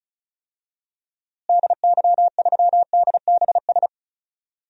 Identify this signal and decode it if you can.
Morse code, heard as a keyed tone, about 36 wpm, 700 Hz. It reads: DY3DBS
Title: DXCC-100 Call Signs - 6 Characters - Repeated 2x - 35wpm